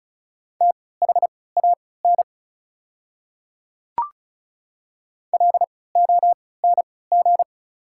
Morse code, heard as a keyed tone, about 34 wpm, spaced as for 20 wpm, 690 Hz. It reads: THAN LONG
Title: 3x Character Spacing - Top 300 Words - 35wpm